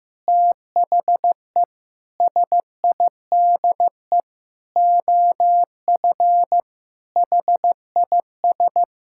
Morse code, heard as a keyed tone, about 15 wpm, 700 Hz. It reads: THE SIDE OF HIS
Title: Sentences from Top 1600 Words 15wpm